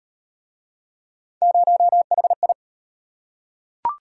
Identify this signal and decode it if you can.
Morse code, heard as a keyed tone, about 33 wpm, 700 Hz. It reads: 0HI